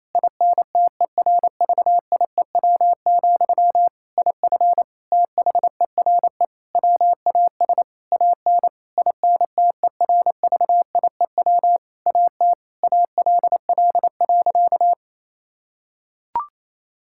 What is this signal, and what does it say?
Morse code, heard as a keyed tone, about 28 wpm, 700 Hz. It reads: INTERVIEW, IF THERE WAS AN INTERVIEW AT ALL.
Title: Sentences from Top 1600 Words 28wpm